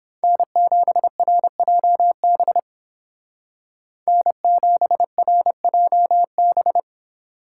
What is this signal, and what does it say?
Morse code, heard as a keyed tone, about 30 wpm, 700 Hz. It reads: N7RJB N7RJB